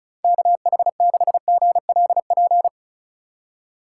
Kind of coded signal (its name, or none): Morse code